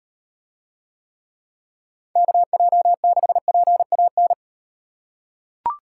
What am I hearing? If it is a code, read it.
Morse code, heard as a keyed tone, about 38 wpm, 700 Hz. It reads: KJ6PAN